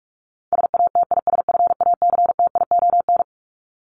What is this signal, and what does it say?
Morse code, heard as a keyed tone, about 44 wpm, 700 Hz. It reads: SATISFACTION
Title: Top 1751-2000 Words 45wpm